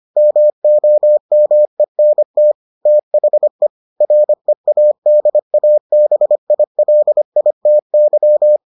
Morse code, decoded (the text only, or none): MOMENT THE READABILITY